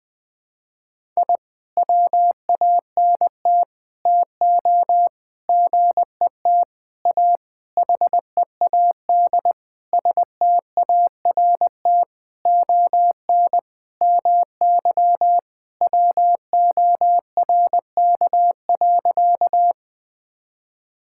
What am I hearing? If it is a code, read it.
Morse code, heard as a keyed tone, about 20 wpm, 700 Hz. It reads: I WANT TO GET A HEAD START ON MY WORK.
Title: Sentences with 7-Word Phrases - 20wpm